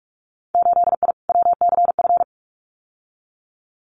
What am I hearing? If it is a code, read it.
Morse code, heard as a keyed tone, about 46 wpm, 700 Hz. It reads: 8I WCF
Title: Winter Field Day Exchanges Encore 45wpm